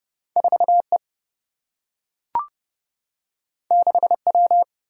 Morse code, heard as a keyed tone, about 30 wpm, 700 Hz. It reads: 4 E 6 W
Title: Single-Letter Single-Number 30wpm